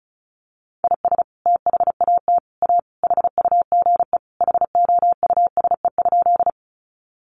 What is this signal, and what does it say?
Morse code, heard as a keyed tone, about 34 wpm, 700 Hz. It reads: IS THAT A HUGE HOUSE?
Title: Sentences from Top 1100 Words 35wpm